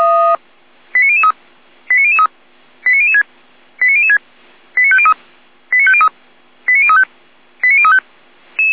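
Five-tone selective call (ZVEI), heard as quick runs of five tones, tones of about 70 ms. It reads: 89003 89003 89006 89006 89503 89503 89035 89035
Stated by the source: ZVEI five-tone call received by SDR.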